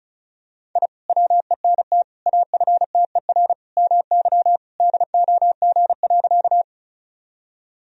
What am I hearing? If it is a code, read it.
Morse code, heard as a keyed tone, about 36 wpm, 700 Hz. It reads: I WENT AFTER MY DOG.
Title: Sentences from Top 1000 Words - Review Part 2 - 35wpm